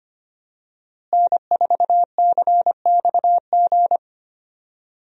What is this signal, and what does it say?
Morse code, heard as a keyed tone, about 25 wpm, 700 Hz. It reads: N4CXG
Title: USA Call Signs - 5 Characters - Repeated 3x - 25wpm